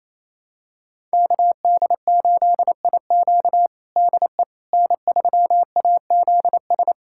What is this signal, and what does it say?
Morse code, heard as a keyed tone, about 28 wpm, 700 Hz. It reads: KD8IQ DE N3AZS
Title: USA Callsign DE Callsign - 5 Characters - 28wpm